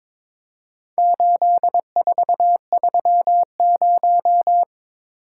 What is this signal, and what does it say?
Morse code, heard as a keyed tone, about 22 wpm, 700 Hz. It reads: 8430
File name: practice / 22wpm / Four Letter-Number Combinations - 22wpm